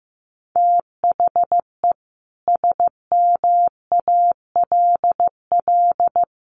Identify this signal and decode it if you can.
Morse code, heard as a keyed tone, about 15 wpm, 700 Hz. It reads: THE SMALL
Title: Sentences from Top 1400 Words 15wpm